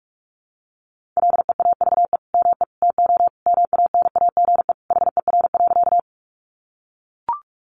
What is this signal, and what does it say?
Morse code, heard as a keyed tone, about 45 wpm, 700 Hz. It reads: LEAVE ME TO MANAGE HER.
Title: Sentences from Top 1400 Words 45wpm